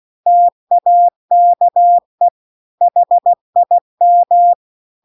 Morse code, heard as a keyed tone, about 16 wpm, 700 Hz. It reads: TAKE HIM
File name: CW-The-Big-Twoheaded-River-2-16WPM